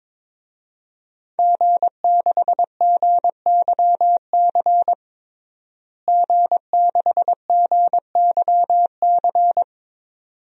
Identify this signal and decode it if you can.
Morse code, heard as a keyed tone, about 21 wpm, 700 Hz. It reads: G6GYC G6GYC
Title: DXCC-100 Call Signs - 5 Characters - Repeated 2x - 22wpm